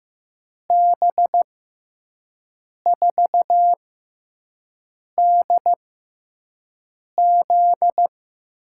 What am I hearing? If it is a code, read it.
Morse code, heard as a keyed tone, about 15 wpm, spaced as for 5 wpm, 700 Hz. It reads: B4DZ
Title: USA Call Signs - 6x Character Spacing - 15wpm